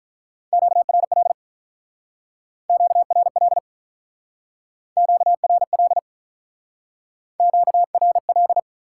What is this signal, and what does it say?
Morse code, heard as a keyed tone, about 44 wpm, 700 Hz. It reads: QRL QRL QRL QRL
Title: Speed-Racing Top 90 QSO Elements 35wpm